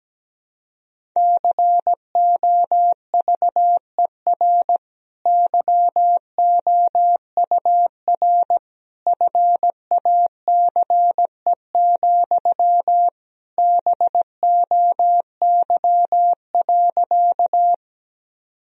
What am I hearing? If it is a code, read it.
Morse code, heard as a keyed tone, about 17 wpm, 700 Hz. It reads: COVER YOUR FACE, BOY.
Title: Sentences from Top 1000 Words - Review Part 1 - 17wpm